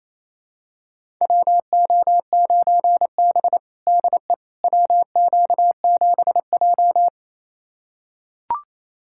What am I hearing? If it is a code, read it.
Morse code, heard as a keyed tone, about 28 wpm, 700 Hz. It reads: WO9B DE WQ7J
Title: USA Callsign DE Callsign - 4 Characters - 28wpm